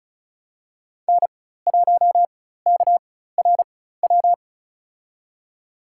Morse code, heard as a keyed tone, about 36 wpm, spaced as for 16 wpm, 700 Hz. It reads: N1KRW
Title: USA Call Signs - 5 Characters - 4x Character Spacing - 35wpm